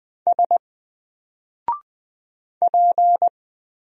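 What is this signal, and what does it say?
Morse code, heard as a keyed tone, about 20 wpm, 700 Hz. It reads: S P
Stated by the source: Single letter and number Morse drill at 20 wpm, from N4LCD.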